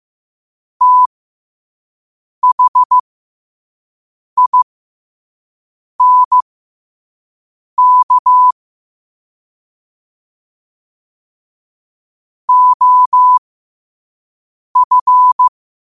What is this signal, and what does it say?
Morse code, heard as a keyed tone, about 15 wpm, spaced as for 5 wpm, 1000 Hz. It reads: THINK OF
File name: 5wpm